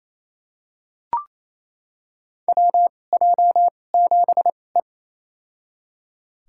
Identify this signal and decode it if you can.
Morse code, heard as a keyed tone, about 28 wpm, spaced as for 20 wpm, 700 Hz. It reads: WJ7E
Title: USA Call Signs - 4 Characters - 2x Character Spacing - 28wpm